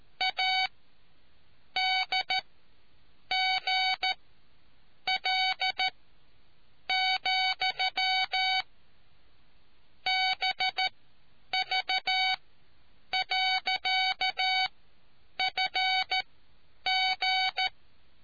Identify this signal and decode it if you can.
Morse code, heard as a keyed tone, about 13 wpm, spaced as for 7 wpm, 750 Hz. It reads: ADGL, BV.FG